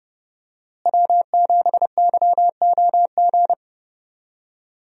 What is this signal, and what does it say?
Morse code, heard as a keyed tone, about 30 wpm, 700 Hz. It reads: W7YOG
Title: USA Call Signs - 5 Characters - Repeated 3x - 30wpm